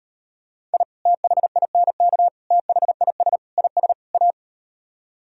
Morse code, heard as a keyed tone, about 38 wpm, 700 Hz. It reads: I THINK THIS IS A